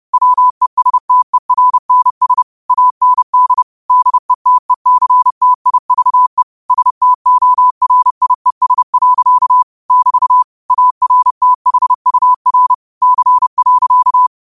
Morse code, heard as a keyed tone, about 30 wpm, 1000 Hz. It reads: WESTERNS AND DETECTIVE STORIES. <BT> ARTHUR C.